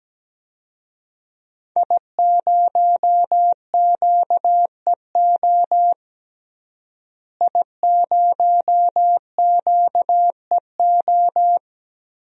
Morse code, heard as a keyed tone, about 17 wpm, 700 Hz. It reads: I0QEO I0QEO
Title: DXCC-100 Call Signs - 5 Characters - Repeated 3x - 17wpm